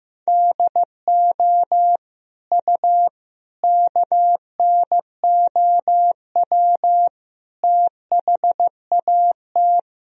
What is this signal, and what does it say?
Morse code, heard as a keyed tone, about 15 wpm, 700 Hz. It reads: DO U KNOW THAT
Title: Sentences from Top 400 Words and Abbreviations 15wpm